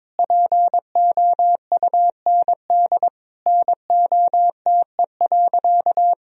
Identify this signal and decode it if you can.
Morse code, heard as a keyed tone, about 22 wpm, 700 Hz. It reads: POUND NOTE.
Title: Sentences from Top 1000 Words - Review Part 2 - 22wpm